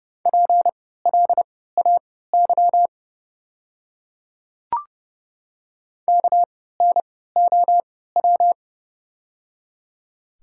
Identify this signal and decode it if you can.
Morse code, heard as a keyed tone, about 29 wpm, spaced as for 17 wpm, 690 Hz. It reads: PLAY KNOW
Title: 3x Character Spacing - Top 200 Words - 30wpm